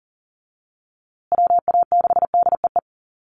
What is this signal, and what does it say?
Morse code, heard as a keyed tone, about 40 wpm, 700 Hz. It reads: WA6DEE
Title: USA Call Signs Repeated 2x 40wpm